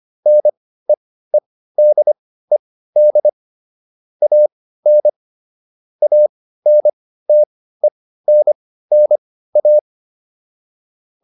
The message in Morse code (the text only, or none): NEEDED AN ANTENNA